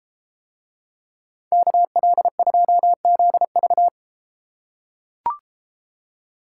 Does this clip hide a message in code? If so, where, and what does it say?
Yes: Morse code from 1.5 to 3.9 s, heard as a keyed tone, about 32 wpm, 700 Hz. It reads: KL2ZV